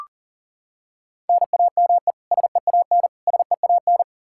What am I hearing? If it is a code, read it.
Morse code, heard as a keyed tone, about 40 wpm, 700 Hz. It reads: NAME SEAN SEAN
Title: Practice QSO - Exchange 1 - Beginning - 40wpm